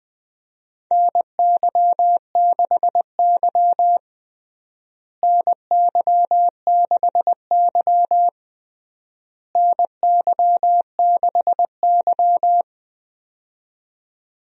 Morse code, heard as a keyed tone, about 20 wpm, 700 Hz. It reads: NY6Y NY6Y NY6Y
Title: USA Call Signs - 4 Characters - Repeated 3x - 20wpm